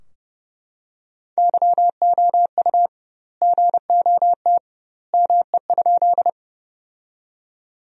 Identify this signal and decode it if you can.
Morse code, heard as a keyed tone, about 32 wpm, 700 Hz. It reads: YOU GOT ME?
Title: 2x Word-Spacing - Sentences from Top 300 Words - 30wpm